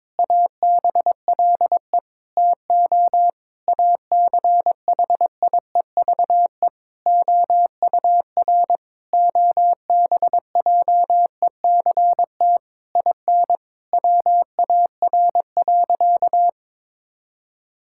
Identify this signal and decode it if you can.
Morse code, heard as a keyed tone, about 22 wpm, 700 Hz. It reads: ABLE TO ACHIEVE OUR OBJECT IN WAR.
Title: Sentences from Top 1000 Words - Review Part 6 - 22wpm